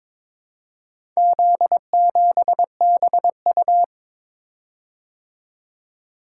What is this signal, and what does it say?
Morse code, heard as a keyed tone, about 21 wpm, 700 Hz. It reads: Z7BU
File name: practice / 22wpm / Four Letter-Number Combinations - 22wpm